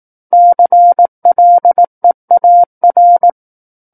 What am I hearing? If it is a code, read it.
Morse code, heard as a keyed tone, about 18 wpm, 700 Hz. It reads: CLEAR
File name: 18 WPM